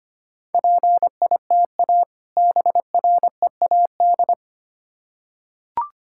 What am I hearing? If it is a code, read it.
Morse code, heard as a keyed tone, about 25 wpm, 700 Hz. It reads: PITA BREAD
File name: practice / 25wpm / Difficult Words with Hints - 25wpm